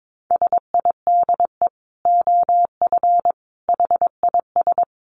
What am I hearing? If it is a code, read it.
Morse code, heard as a keyed tone, about 22 wpm, 700 Hz. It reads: SIDE OF HIS
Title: Dit Challenge 22wpm